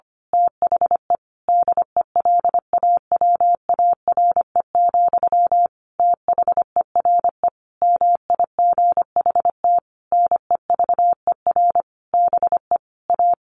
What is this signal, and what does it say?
Morse code, heard as a keyed tone, about 25 wpm, 700 Hz. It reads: THE DELAWARE, THERE MIGHT NEVER BE A
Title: A Sound Of Thunder 25wpm